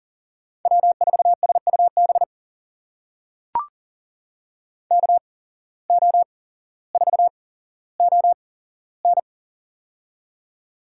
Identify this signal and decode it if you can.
Morse code, heard as a keyed tone, about 37 wpm, spaced as for 17 wpm, 700 Hz. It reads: W4SUB KO4ON
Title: USA Call Signs - 5 Characters - 8x Character Spacing - 40wpm